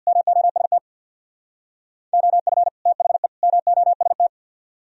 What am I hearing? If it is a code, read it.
Morse code, heard as a keyed tone, about 67 wpm, 700 Hz. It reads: MOST OF THE MOST